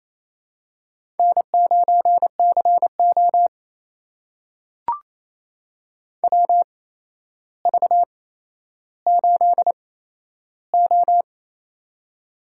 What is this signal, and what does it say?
Morse code, heard as a keyed tone, about 27 wpm, spaced as for 12 wpm, 700 Hz. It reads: N9CO WV8O
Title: USA Call Signs - 4 Characters - 8x Character Spacing - 28wpm